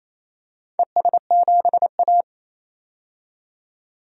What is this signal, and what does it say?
Morse code, heard as a keyed tone, about 28 wpm, 700 Hz. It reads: ES7A
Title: DXCC-100 Call Signs - 4 Characters - 28wpm